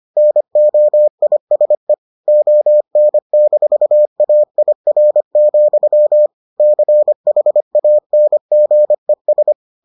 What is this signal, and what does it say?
Morse code, heard as a keyed tone, about 25 wpm, 600 Hz. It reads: NOISE ON-AIR, CHANGES